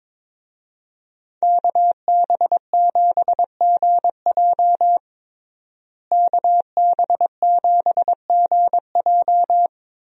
Morse code, heard as a keyed tone, about 22 wpm, 700 Hz. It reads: KB7GJ KB7GJ
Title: USA Call Signs - 5 Characters - Repeated 3x - 22wpm